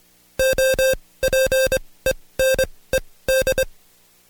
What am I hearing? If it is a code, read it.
Morse code, heard as a keyed tone, about 24 wpm, spaced as for 17 wpm, 530 Hz. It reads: OPENED
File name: Code Quick / AudioDownld2 / Seg135-Stepper8 17WPM